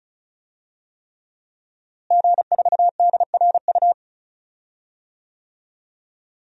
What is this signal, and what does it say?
Morse code, heard as a keyed tone, about 35 wpm, 700 Hz. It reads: G4DRU
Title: DXCC-100 Call Signs - 5 Characters - Repeated 3x - 35wpm